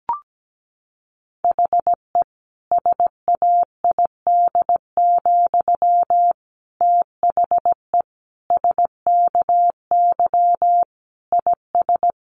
Morse code, heard as a keyed tone, about 17 wpm, 700 Hz. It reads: HE SAID, THE SKY IS
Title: Sentences from Top 1200 Words 17wpm